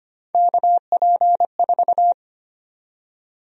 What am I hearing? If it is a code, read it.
Morse code, heard as a keyed tone, about 25 wpm, 700 Hz. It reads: KP4
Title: USA Call Sign Prefixes - 25wpm